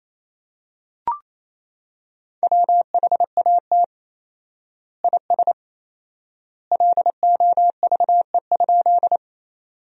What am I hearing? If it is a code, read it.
Morse code, heard as a keyed tone, about 28 wpm, 700 Hz. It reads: WHAT IS LOVE?
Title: 4x Word-Spacing - Sentences from Top 400 Words - 28wpm